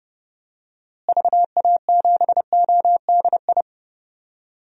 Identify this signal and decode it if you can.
Morse code, heard as a keyed tone, about 31 wpm, 700 Hz. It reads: VA7ODI
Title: DXCC-100 Call Signs - 6 Characters - Repeated 2x - 30wpm